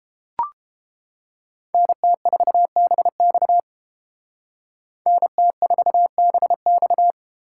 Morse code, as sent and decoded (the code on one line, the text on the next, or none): -. - ....- -... -..- / -. - ....- -... -..-
NT4BX NT4BX